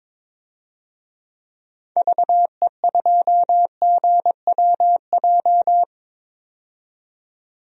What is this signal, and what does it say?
Morse code, heard as a keyed tone, about 22 wpm, 700 Hz. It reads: VE2GWJ
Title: DXCC-100 Call Signs - 6 Characters - 22wpm